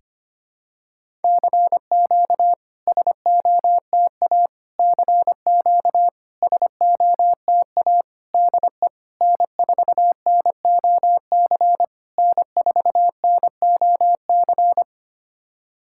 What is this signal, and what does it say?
Morse code, heard as a keyed tone, about 25 wpm, 700 Hz. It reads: CQ SOTA CQ SOTA DE N4NOC N4NOC
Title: SOTA QSO Practice 25wpm